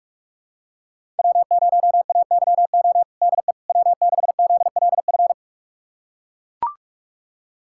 Morse code, heard as a keyed tone, about 41 wpm, 700 Hz. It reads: W0AYO DE W6ZLF
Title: USA Callsign DE Callsign - 5 Characters - 45wpm